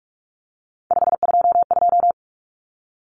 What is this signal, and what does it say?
Morse code, heard as a keyed tone, about 44 wpm, 700 Hz. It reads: 522
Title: Common Cut Numbers - 3-Digit 45wpm